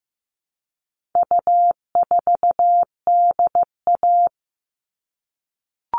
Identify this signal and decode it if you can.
Morse code, heard as a keyed tone, about 15 wpm, 700 Hz. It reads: U4DA
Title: All Cut Numbers - 4-Digit 15wpm